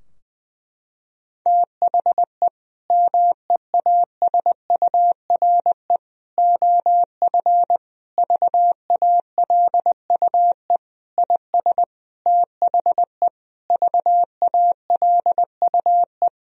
Morse code, heard as a keyed tone, about 20 wpm, 700 Hz. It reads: THE MEASURE OF VALUE IS THE VALUE
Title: Sentences from Top 1000 Words - Review Part 3 - 20wpm